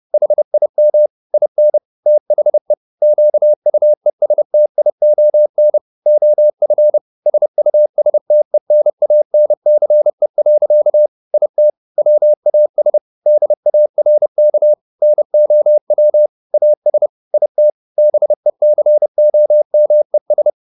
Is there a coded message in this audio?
Yes: Morse code from 0.1 to 20.5 s, heard as a keyed tone, about 30 wpm, 600 Hz. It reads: HIM IN THE QUESTION OF SUSTENANCE. IT WAS DARK NOW AS IT BECOMES